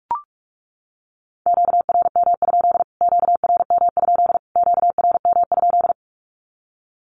Morse code, heard as a keyed tone, about 44 wpm, 700 Hz. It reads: QRM? QRM? QRM?
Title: Q-Codes Repeated 3x 45wpm